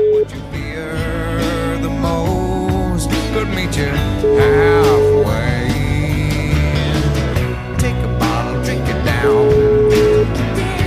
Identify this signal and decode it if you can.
Ringback tone (Europe, 425 Hz), with noise.